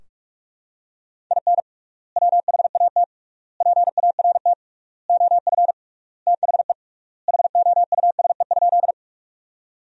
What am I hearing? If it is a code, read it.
Morse code, heard as a keyed tone, about 46 wpm, 700 Hz. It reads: IN WHAT PART OF THE HOUSE?
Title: 3x Word-Spacing - Sentences from Top 200 Words - 45wpm